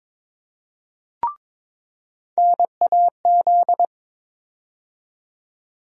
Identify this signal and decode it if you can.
Morse code, heard as a keyed tone, about 22 wpm, 700 Hz. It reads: NAZ
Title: USA Call Sign Postfixes - 3 Letters 22wpm